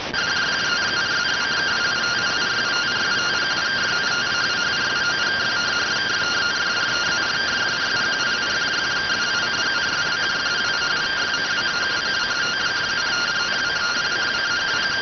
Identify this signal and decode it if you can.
Two-tone teleprinter signal, about 170 Hz shift.
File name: findings / CT AXIAL -572mm 2.gen-34 rx